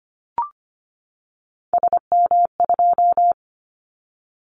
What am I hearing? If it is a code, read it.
Morse code, heard as a keyed tone, about 25 wpm, 700 Hz. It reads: SM2
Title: Callsign Prefixes for Top 25 DXCC - 25wpm